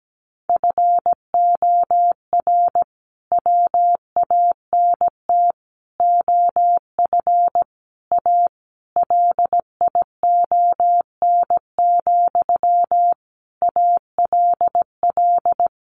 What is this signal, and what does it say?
Morse code, heard as a keyed tone, about 17 wpm, 700 Hz. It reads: FOR WANT OF A LION, ALL
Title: A Sound Of Thunder 17wpm